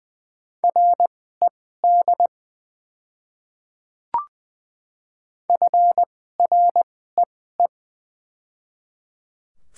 Morse code, heard as a keyed tone, about 20 wpm, spaced as for 14 wpm, 700 Hz. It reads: RED FREE
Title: 2x Character Spacing - Top 500 Words - 20wpm